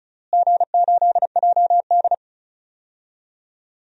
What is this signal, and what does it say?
Morse code, heard as a keyed tone, about 34 wpm, 700 Hz. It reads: G8JD
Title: DXCC-100 Call Signs - 4 Characters - Repeated 2x - 35wpm